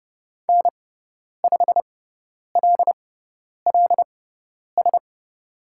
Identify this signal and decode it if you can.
Morse code, heard as a keyed tone, about 29 wpm, spaced as for 10 wpm, 700 Hz. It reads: N5LLS
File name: Lesson 16g - Letter c - Call Signs 30wpm